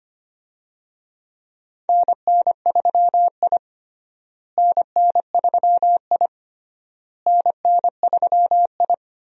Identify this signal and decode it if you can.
Morse code, heard as a keyed tone, about 25 wpm, 700 Hz. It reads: NN3I NN3I NN3I
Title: USA Call Signs - 4 Characters - Repeated 3x - 25wpm